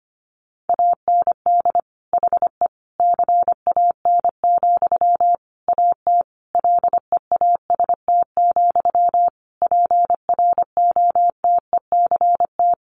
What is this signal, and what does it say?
Morse code, heard as a keyed tone, about 25 wpm, 700 Hz. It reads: AND HE CAN, AT LEAST, PROTECT
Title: Sentences from Top 900 Words 25wpm